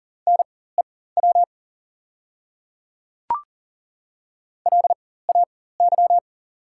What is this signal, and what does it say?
Morse code, heard as a keyed tone, about 39 wpm, spaced as for 19 wpm, 700 Hz. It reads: NEW LAY
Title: 4x Character Spacing - Top 500 Words - 40wpm